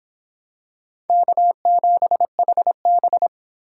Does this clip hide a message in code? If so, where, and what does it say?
Yes: Morse code from 1.1 to 3.3 s, heard as a keyed tone, about 25 wpm, 700 Hz. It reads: K7HB